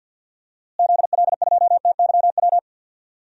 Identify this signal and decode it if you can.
Morse code, heard as a keyed tone, about 53 wpm, 700 Hz. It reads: ZL2TYW
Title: DXCC-100 Call Signs - 6 Characters - Repeated 2x - 50wpm